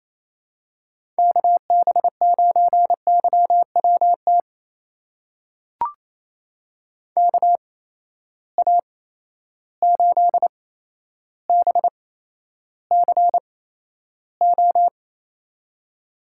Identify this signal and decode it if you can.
Morse code, heard as a keyed tone, about 28 wpm, spaced as for 12 wpm, 700 Hz. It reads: KB9YWT KA8BCO
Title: USA Call Signs - 6 Characters - 8x Character Spacing - 28wpm